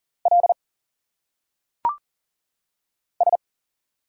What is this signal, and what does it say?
Morse code, heard as a keyed tone, about 40 wpm, 700 Hz. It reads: L S
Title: Single Letters - Phonetic Pronunciation 40wpm